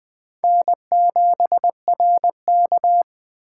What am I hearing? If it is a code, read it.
Morse code, heard as a keyed tone, about 21 wpm, 700 Hz. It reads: N7RK
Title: Speed Racing - USA Call Signs 17wpm